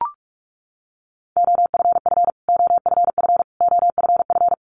Morse code, heard as a keyed tone, about 44 wpm, 700 Hz. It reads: OFF OFF OFF
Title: Top 500 Words Repeated 45wpm